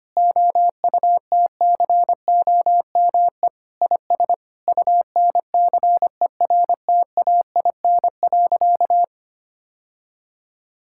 Morse code, heard as a keyed tone, about 25 wpm, 700 Hz. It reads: OUTCOME IS UNCERTAIN.
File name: practice / 25wpm / Sentences with 3-Word Phrases - 25wpm